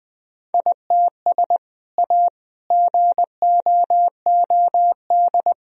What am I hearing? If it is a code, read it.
Morse code, heard as a keyed tone, about 20 wpm, 700 Hz. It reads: ITS A GOOD
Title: Sentences from Top 1000 Words - Review Part 1 - 20wpm